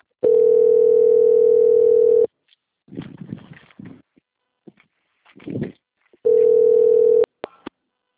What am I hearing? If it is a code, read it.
Ringback tone (North America), 440 + 480 Hz.